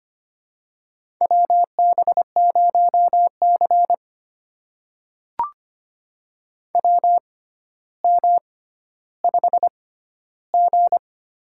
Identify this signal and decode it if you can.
Morse code, heard as a keyed tone, about 25 wpm, spaced as for 13 wpm, 700 Hz. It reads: WB0C WM5G
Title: USA Call Signs - 4 Characters - 6x Character Spacing - 25wpm